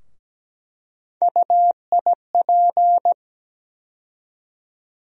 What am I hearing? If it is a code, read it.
Morse code, heard as a keyed tone, about 17 wpm, 700 Hz. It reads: UIP
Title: USA Call Sign Postfixes - 3 Letters 17wpm